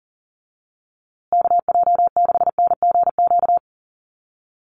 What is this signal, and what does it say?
Morse code, heard as a keyed tone, about 40 wpm, 700 Hz. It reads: KJ6NGQ
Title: USA Call Signs Repeated 2x 40wpm